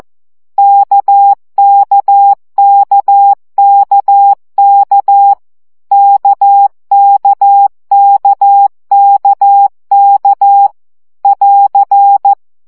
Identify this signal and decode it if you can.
Morse code, heard as a keyed tone, about 14 wpm, 800 Hz. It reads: KKKKK KKKKK <AR>
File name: CW gyakorlas k 12 baud